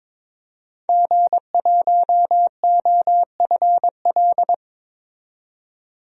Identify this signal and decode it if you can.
Morse code, heard as a keyed tone, about 22 wpm, 700 Hz. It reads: G1OFL
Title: DXCC-100 Call Signs - 5 Characters - 22wpm